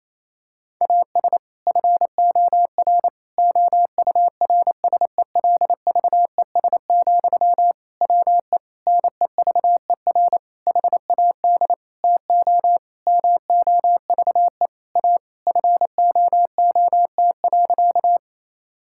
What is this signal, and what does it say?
Morse code, heard as a keyed tone, about 28 wpm, 700 Hz. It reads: AS FOR OURSELVES, WE NEVER HAD TO MOVE A FOOT.
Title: Sentences from Top 1600 Words 28wpm